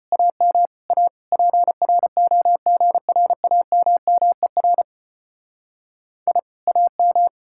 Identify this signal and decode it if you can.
Morse code, heard as a keyed tone, about 31 wpm, 700 Hz. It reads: AM A PROGRAMMER I AM
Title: Speed-Racing Common QSO Phrases - 25wpm